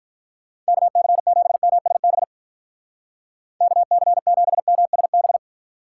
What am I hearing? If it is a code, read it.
Morse code, heard as a keyed tone, about 44 wpm, 700 Hz. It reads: KC7MSB KC7MSB
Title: Speed Racing - USA 2x3 Call Signs 35wpm